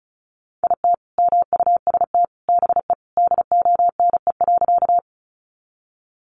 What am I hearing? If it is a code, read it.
Morse code, heard as a keyed tone, about 34 wpm, 700 Hz. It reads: IT MUST BE DONE.
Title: Sentences from Top 500 Words 35wpm